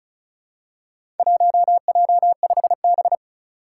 Morse code, heard as a keyed tone, about 34 wpm, 700 Hz. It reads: 1J5B
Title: Four Challenging Letter-Number Combinations 35wpm